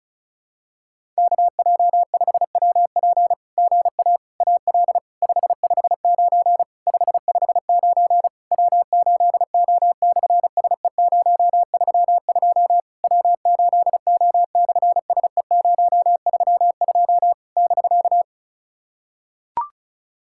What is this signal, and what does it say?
Morse code, heard as a keyed tone, about 33 wpm, 700 Hz. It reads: KJ5WP GA AL 559 559 W8O/SE032 W8O/SE032 <BK>
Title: SOTA QSO Practice 35wpm